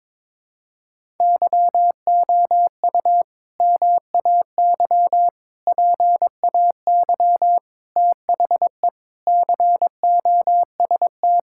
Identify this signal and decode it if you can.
Morse code, heard as a keyed tone, about 22 wpm, 700 Hz. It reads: YOU MAY PAY THE COST
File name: practice / 22wpm / Sentences with 8-Word Phrases - 22wpm